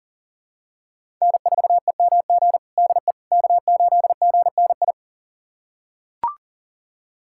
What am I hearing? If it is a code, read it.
Morse code, heard as a keyed tone, about 40 wpm, 700 Hz. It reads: N4EMG DE K8GNI
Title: USA Callsign DE Callsign - 5 Characters - 40wpm